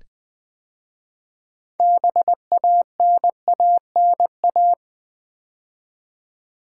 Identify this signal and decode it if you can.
Morse code, heard as a keyed tone, about 20 wpm, 700 Hz. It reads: BANANA
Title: 6 Letter Words - First 3 Letters Given - 20wpm